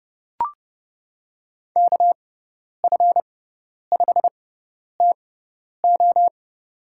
Morse code, heard as a keyed tone, about 30 wpm, spaced as for 10 wpm, 700 Hz. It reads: KF5TO
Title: USA Call Signs - 5 Characters - 6x Character Spacing - 30wpm